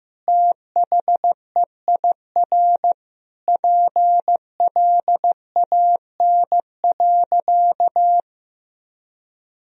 Morse code, heard as a keyed tone, about 15 wpm, 700 Hz. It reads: THEIR PLAN.
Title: Sentences from Top 1000 Words - Review Part 2 - 15wpm